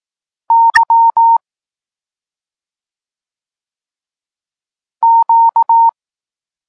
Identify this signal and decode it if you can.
Morse code, heard as a keyed tone, about 18 wpm, 920 Hz. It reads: Y Q
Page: Tekst 32 – Doskonalenie odbioru poznanych liter